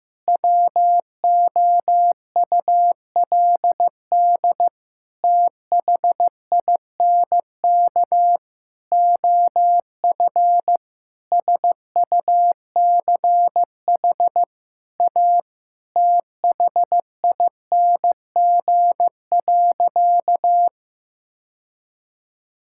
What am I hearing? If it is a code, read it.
Morse code, heard as a keyed tone, about 15 wpm, 700 Hz. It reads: WOULD THINK OF SUCH A THING.
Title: Sentences from Top 1600 Words 15wpm